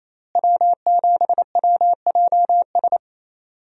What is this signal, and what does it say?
Morse code, heard as a keyed tone, about 28 wpm, 700 Hz. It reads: W7WJS
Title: USA Call Signs - 5 Characters - Repeated 3x - 28wpm